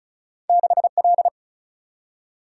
Morse code, heard as a keyed tone, about 34 wpm, 700 Hz. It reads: 6L
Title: Single-Letter Single-Number 35wpm